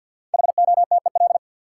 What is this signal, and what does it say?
Morse code, heard as a keyed tone, about 53 wpm, 700 Hz. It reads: HOTEL
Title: Sentences from Top 1600 Words - New Word Drill - 50wpm